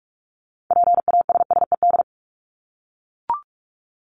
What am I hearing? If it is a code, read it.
Morse code, heard as a keyed tone, about 44 wpm, 700 Hz. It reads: PASSED
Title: ED Suffix - Top 100 Words with Base Words 45wpm